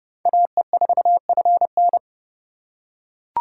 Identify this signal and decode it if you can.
Morse code, heard as a keyed tone, about 30 wpm, 700 Hz. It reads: AE4FN